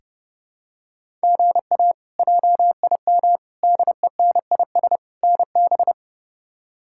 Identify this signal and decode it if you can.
Morse code, heard as a keyed tone, about 30 wpm, 700 Hz. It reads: GA JIM DENIS NB
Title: K1USN SST QSO Practice - 30wpm